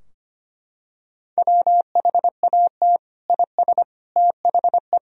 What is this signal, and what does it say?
Morse code, heard as a keyed tone, about 25 wpm, 700 Hz. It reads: WHAT IS THE
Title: Sentences from Top 1000 Words - Review Part 4 - 25wpm